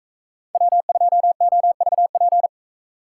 Morse code, heard as a keyed tone, about 33 wpm, 700 Hz. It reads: W2OVP
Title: Speed Racing - USA 1x3 Call Signs 28wpm